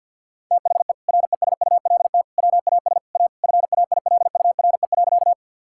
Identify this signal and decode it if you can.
Morse code, heard as a keyed tone, about 50 wpm, 700 Hz. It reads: THE RESULT WAS A FAILURE.
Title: Sentences from Top 1000 Words - Review Part 6 - 50wpm